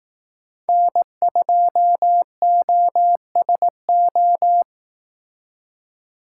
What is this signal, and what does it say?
Morse code, heard as a keyed tone, about 18 wpm, 700 Hz. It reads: N2OSO
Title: Speed Racing - USA 1x3 Call Signs 15wpm